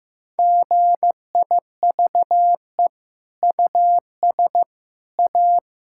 Morse code, heard as a keyed tone, about 15 wpm, 700 Hz. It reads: GIVE US A
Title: Sentences from Top 1600 Words 15wpm